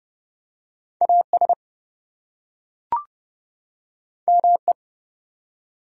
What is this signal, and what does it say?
Morse code, heard as a keyed tone, about 30 wpm, 700 Hz. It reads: AS ME
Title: Two Letter Words - 30wpm